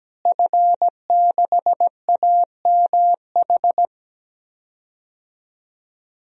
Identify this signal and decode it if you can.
Morse code, heard as a keyed tone, about 17 wpm, 700 Hz. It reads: F6AMH
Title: DXCC-100 Call Signs - 5 Characters - Repeated 2x - 17wpm